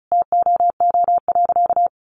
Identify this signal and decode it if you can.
Morse code, heard as a keyed tone, about 34 wpm, 700 Hz. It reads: TOO.
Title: Sentences from Top 700 Words 35wpm